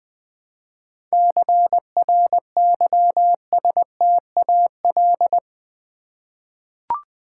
Morse code, heard as a keyed tone, about 20 wpm, 700 Hz. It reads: CRYSTAL
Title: 7 Letter Words - First 3 Letters Given - 20wpm